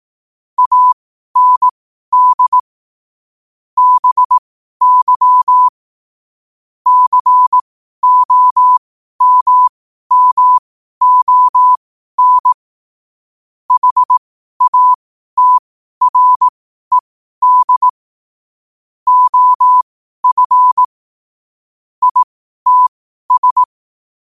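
Morse code, heard as a keyed tone, about 18 wpm, spaced as for 12 wpm, 1000 Hz. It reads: AND BY COMMON HATRED OF ITS
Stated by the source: Morse code quote of the day at 12 wpm.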